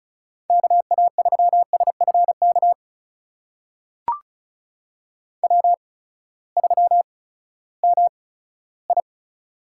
Morse code, heard as a keyed tone, about 34 wpm, spaced as for 11 wpm, 700 Hz. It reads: KA3SF KW3MI
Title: USA Call Signs - 8x Character Spacing - 35wpm